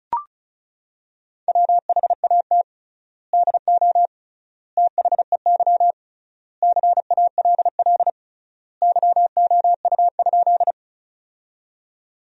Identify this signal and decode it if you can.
Morse code, heard as a keyed tone, about 36 wpm, 700 Hz. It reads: WHAT DO THEY CALL YOU?
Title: 3x Word-Spacing - Sentences from Top 100 Words - 35wpm